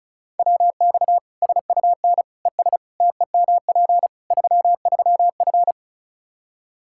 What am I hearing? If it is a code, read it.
Morse code, heard as a keyed tone, about 36 wpm, 700 Hz. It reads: WX SUN ES TEMP 33F
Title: K1USN SST QSO Practice 35wpm